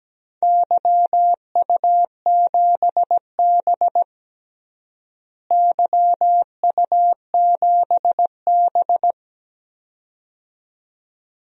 Morse code, heard as a keyed tone, about 17 wpm, 700 Hz. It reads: YU7B YU7B
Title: DXCC-100 Call Signs - 4 Characters - Repeated 2x - 17wpm